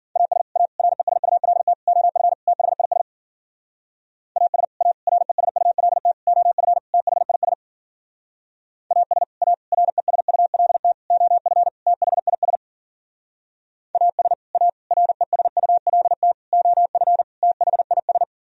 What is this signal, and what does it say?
Morse code, heard as a keyed tone, about 51 wpm, 700 Hz. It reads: AS A RESULT OF THIS AS A RESULT OF THIS AS A RESULT OF THIS AS A RESULT OF THIS